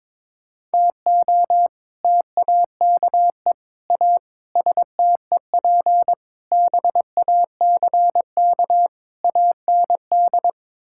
Morse code, heard as a keyed tone, about 21 wpm, 700 Hz. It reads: TO TAKE A STEP BACK AND
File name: practice / 22wpm / Sets of 6 Words - 22wpm